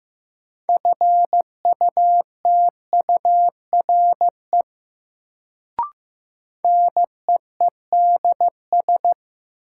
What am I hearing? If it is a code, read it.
Morse code, heard as a keyed tone, about 15 wpm, 700 Hz. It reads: FUTURE NEEDS
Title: Sets of 3 Words - New Word Drill - 15wpm